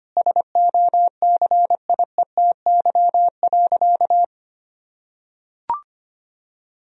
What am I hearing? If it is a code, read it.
Morse code, heard as a keyed tone, about 25 wpm, 700 Hz. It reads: SOCIETY.
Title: Sentences from Top 1000 Words - Review Part 4 - 25wpm